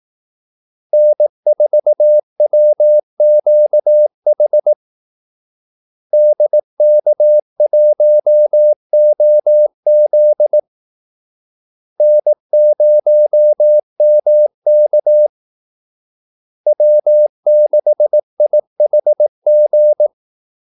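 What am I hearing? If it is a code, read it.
Morse code, heard as a keyed tone, about 18 wpm, 600 Hz. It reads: N4WQH DK1OZ N0MK W6IHG
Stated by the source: Morse call-sign practice at 18 wpm.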